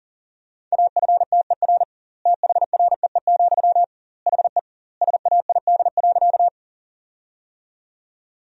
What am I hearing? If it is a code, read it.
Morse code, heard as a keyed tone, about 40 wpm, 700 Hz. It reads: AFTER THREE, HE SAID.
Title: 2x Word-Spacing - Sentences from Top 200 Words - 40wpm